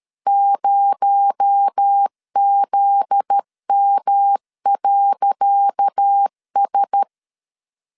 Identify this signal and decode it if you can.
Morse code, heard as a keyed tone, about 13 wpm, 800 Hz. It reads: 0ZM.S
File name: LEZIONI / 10 SIGNS / 10D COMA 12wpm 2544